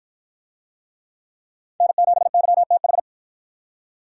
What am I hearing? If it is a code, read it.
Morse code, heard as a keyed tone, about 50 wpm, 700 Hz. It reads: N7YTH